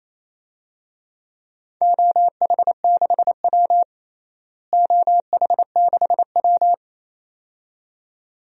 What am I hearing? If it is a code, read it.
Morse code, heard as a keyed tone, about 28 wpm, 700 Hz. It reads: OH6W OH6W
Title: DXCC-100 Call Signs - 4 Characters - Repeated 2x - 28wpm